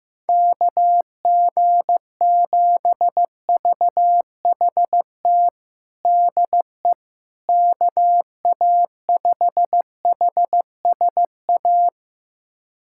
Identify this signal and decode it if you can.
Morse code, heard as a keyed tone, about 15 wpm, 700 Hz. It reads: KG7VHT DE KA5HSA
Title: USA Callsign DE Callsign - 6 Characters - 15wpm